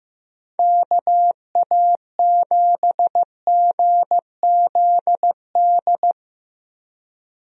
Morse code, heard as a keyed tone, about 15 wpm, 700 Hz. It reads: KA7GZD
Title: USA Call Signs - 6 Characters - Repeated 2x - 15wpm